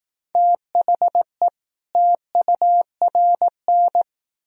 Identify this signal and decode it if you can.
Morse code, heard as a keyed tone, about 20 wpm, 700 Hz. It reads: THE TURN